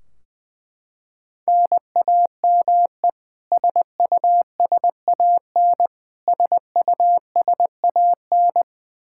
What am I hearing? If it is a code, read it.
Morse code, heard as a keyed tone, about 20 wpm, 700 Hz. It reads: NAME SUSAN SUSAN
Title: Practice QSO - Exchange 1 - Beginning - 20wpm